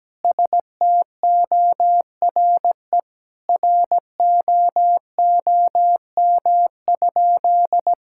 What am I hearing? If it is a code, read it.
Morse code, heard as a keyed tone, about 17 wpm, 700 Hz. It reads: STORE ROOM?
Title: Sentences from Top 1000 Words - Review Part 5 - 17wpm